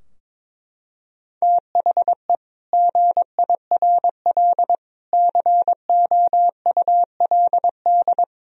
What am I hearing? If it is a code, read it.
Morse code, heard as a keyed tone, about 22 wpm, 700 Hz. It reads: THE GIRL COULD
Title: Sentences from Top 1000 Words - Review Part 4 - 22wpm